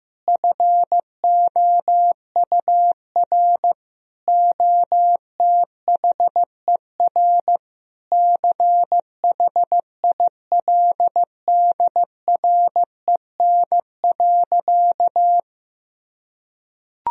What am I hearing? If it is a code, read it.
Morse code, heard as a keyed tone, about 15 wpm, 700 Hz. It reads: FOUR OTHER CHILDREN.
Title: Sentences from Top 1000 Words - Review Part 1 - 15wpm